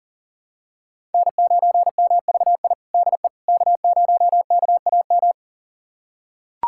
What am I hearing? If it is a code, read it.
Morse code, heard as a keyed tone, about 40 wpm, 700 Hz. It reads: N9MVI DE K0KAM
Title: USA Callsign DE Callsign - 5 Characters - 40wpm